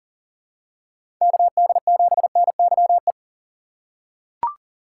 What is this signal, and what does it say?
Morse code, heard as a keyed tone, about 40 wpm, 700 Hz. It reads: KD7NYE
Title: USA Call Signs - 3x Character Spacing - 40wpm